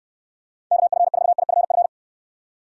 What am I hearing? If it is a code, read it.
Morse code, heard as a keyed tone, about 69 wpm, 700 Hz. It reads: KF4IFF